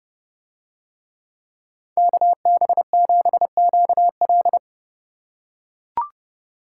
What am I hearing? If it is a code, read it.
Morse code, heard as a keyed tone, about 28 wpm, 700 Hz. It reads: KB7QL